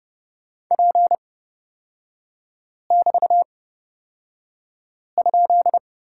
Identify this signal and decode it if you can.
Morse code, heard as a keyed tone, about 30 wpm, spaced as for 5 wpm, 700 Hz. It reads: P<BT>?
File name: Lesson 36e - BT - Three Characters 30wpm